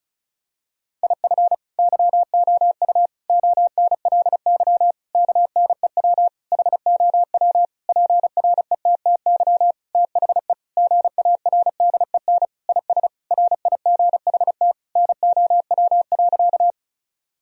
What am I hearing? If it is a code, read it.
Morse code, heard as a keyed tone, about 36 wpm, 700 Hz. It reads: IF YOU ONLY KNEW HOW PRETTY THE GARDEN IS RIGHT NOW.
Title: Sentences from Top 1000 Words - Review Part 4 - 35wpm